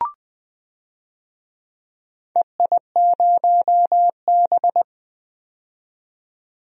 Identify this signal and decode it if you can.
Morse code, heard as a keyed tone, about 21 wpm, 700 Hz. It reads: EI0B
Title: DXCC-100 Call Signs - 4 Characters - 20wpm